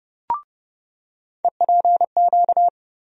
Morse code, heard as a keyed tone, about 31 wpm, 700 Hz. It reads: EPQ
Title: USA Call Sign Postfixes - 3 Letters 30wpm